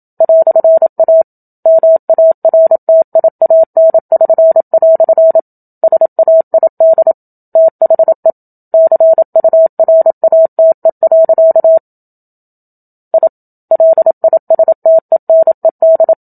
Morse code, heard as a keyed tone, about 27 wpm, 650 Hz. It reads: "A MARTIAN<SN>" SAID THE CURATE. I LISTENED
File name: Book Two / 01-Chapter 1 - 27 WPM